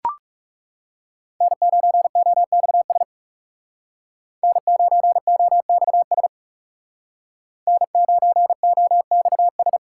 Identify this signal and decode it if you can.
Morse code, heard as a keyed tone, about 34 wpm, 700 Hz. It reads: N9OXS N9OXS N9OXS
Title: Speed Racing - USA 1x3 Call Signs 30wpm